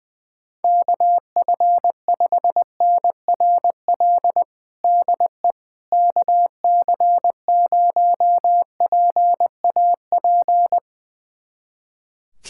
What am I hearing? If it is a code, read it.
Morse code, heard as a keyed tone, about 20 wpm, 700 Hz. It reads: KF5NRL DE KC0PAP
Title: USA Callsign DE Callsign 20wpm